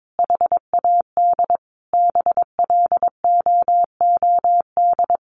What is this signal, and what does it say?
Morse code, heard as a keyed tone, about 22 wpm, 700 Hz. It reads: HAD BLOOD
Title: Sentences from Top 1100 Words 22wpm